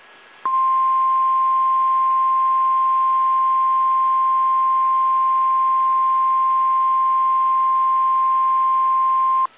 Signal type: Emergency Alert System (SAME) tones